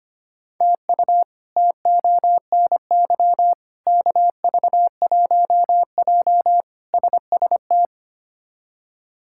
Morse code, heard as a keyed tone, about 25 wpm, 700 Hz. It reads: TU TONY KV1J SST
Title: K1USN SST QSO Practice - 25wpm